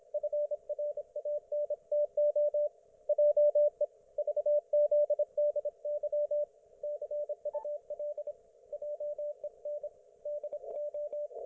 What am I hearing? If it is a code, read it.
Morse code, heard as a keyed tone, about 27 wpm, 590 Hz, with radio noise. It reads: FRANTO JE VZDY CUL JEN DO
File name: Archiv / CUC194 / ok1fhi cul194